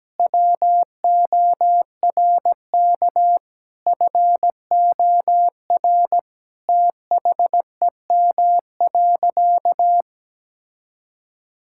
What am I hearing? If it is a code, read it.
Morse code, heard as a keyed tone, about 17 wpm, 700 Hz. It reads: WORK FOR THEM.
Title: Sentences from Top 1000 Words - Review Part 5 - 17wpm